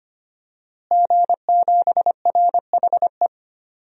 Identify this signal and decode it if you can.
Morse code, heard as a keyed tone, about 25 wpm, 700 Hz. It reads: G7RHE
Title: DXCC-100 Call Signs - 5 Characters - Repeated 3x - 25wpm